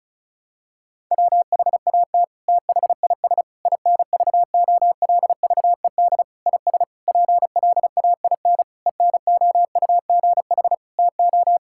Morse code, heard as a keyed tone, about 34 wpm, 700 Hz. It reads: WHAT THIS INVOLVED IS PLAIN ENOUGH TO
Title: Sentences from Top 1000 Words - Review Part 5 - 35wpm